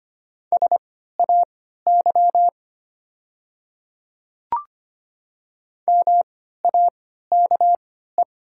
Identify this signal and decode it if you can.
Morse code, heard as a keyed tone, about 25 wpm, spaced as for 14 wpm, 700 Hz. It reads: SAY MAKE
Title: 3x Character Spacing - Top 200 Words - 25wpm